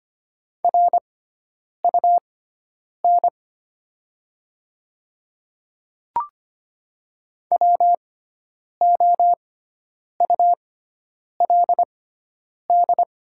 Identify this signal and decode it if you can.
Morse code, heard as a keyed tone, about 25 wpm, spaced as for 9 wpm, 700 Hz. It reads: RUN WOULD
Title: 6x Character Spacing - Top 500 Words - 25wpm